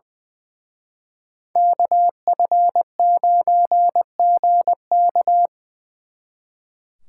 Morse code, heard as a keyed tone, about 23 wpm, 700 Hz. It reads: KF9GK
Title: Speed Racing - USA 2x2 Call Signs 20wpm